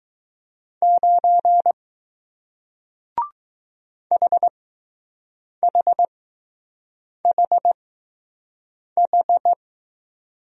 Morse code, heard as a keyed tone, about 20 wpm, spaced as for 6 wpm, 700 Hz. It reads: 9 HHHH